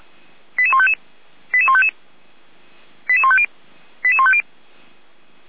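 Five-tone selective call (ZVEI), heard as quick runs of five tones, tones of about 70 ms. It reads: 88155 88155 88150 88150
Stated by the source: ZVEI five-tone selective call received with an SDR.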